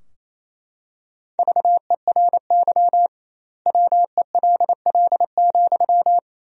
Morse code, heard as a keyed tone, about 28 wpm, 700 Hz. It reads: VERY WELL,
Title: 2x Word-Spacing - Sentences from Top 200 Words - 28wpm